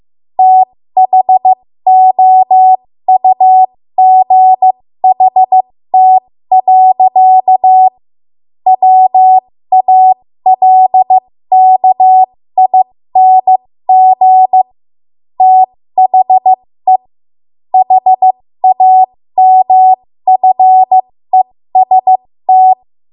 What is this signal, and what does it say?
Morse code, heard as a keyed tone, about 15 wpm, spaced as for 13 wpm, 750 Hz. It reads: THOUGHT. WALKING THE HAMFEST
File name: Morse / 13WPM